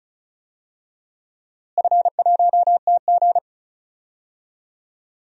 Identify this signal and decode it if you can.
Morse code, heard as a keyed tone, about 36 wpm, 700 Hz. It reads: F1TG
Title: DXCC-100 Call Signs - 4 Characters - Repeated 2x - 35wpm